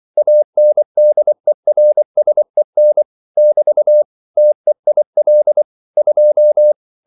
Morse code, heard as a keyed tone, about 24 wpm, 600 Hz. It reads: ANDERSEN <BT> TEIL 2